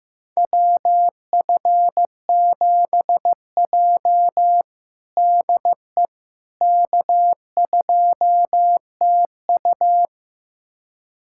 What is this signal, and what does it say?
Morse code, heard as a keyed tone, about 16 wpm, 700 Hz. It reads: WF7J DE K2TU
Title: USA Callsign DE Callsign - 4 Characters - 15wpm